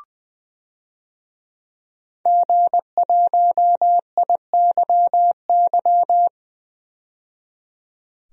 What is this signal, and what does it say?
Morse code, heard as a keyed tone, about 21 wpm, 700 Hz. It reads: G1IYY
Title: DXCC-100 Call Signs - 5 Characters - 20wpm